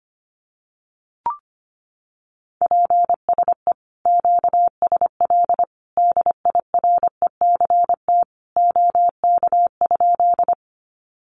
Morse code, heard as a keyed tone, about 25 wpm, 700 Hz. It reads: PSE QSL DIRECT OK?
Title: Common QSO Phrases 25wpm